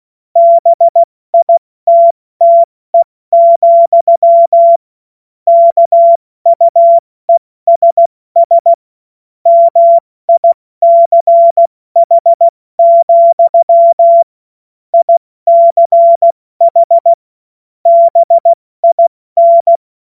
Morse code, heard as a keyed tone, about 16 wpm, 680 Hz. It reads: BITTE, KUESS MICH, ICH BIN
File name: Morsekurs / Witze / Witz 05. April 2022 70 F80